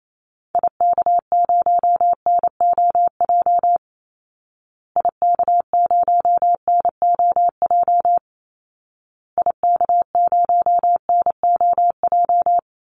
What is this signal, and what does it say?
Morse code, heard as a keyed tone, about 28 wpm, 700 Hz. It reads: IK0NOJ IK0NOJ IK0NOJ
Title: Call Signs Repeated 2x - 28wpm